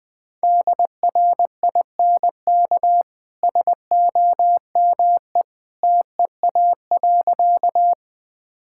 Morse code, heard as a keyed tone, about 20 wpm, 700 Hz. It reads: DRINK SOME TEA.
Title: Easy Sentences with Hints - 20wpm